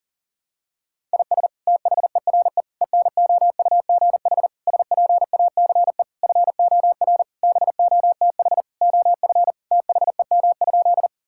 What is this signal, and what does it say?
Morse code, heard as a keyed tone, about 40 wpm, 700 Hz. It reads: IS THERE ENOUGH SPACE FOR BOTH OF THEM?
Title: Sentences from Top 1000 Words - Review Part 5 - 40wpm